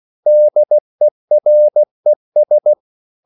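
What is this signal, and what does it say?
Morse code, heard as a keyed tone, about 16 wpm, 600 Hz. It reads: DERES
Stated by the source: Slow (QRS) Morse code at 16 wpm, from CW med Gnister.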